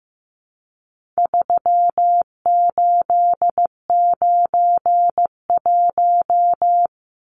Morse code, heard as a keyed tone, about 15 wpm, 700 Hz. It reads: 3891
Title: Numbers - 4-Digit 15wpm